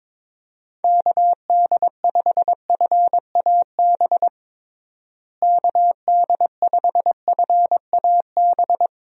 Morse code, heard as a keyed tone, about 22 wpm, 700 Hz. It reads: KD5FAB KD5FAB
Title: USA Call Signs - 6 Characters - Repeated 2x - 22wpm